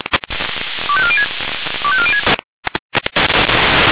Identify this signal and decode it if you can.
Five-tone selective call (ZVEI), heard as quick runs of five tones, tones of about 70 ms. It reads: 25406 25406
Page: SDR/ZVEI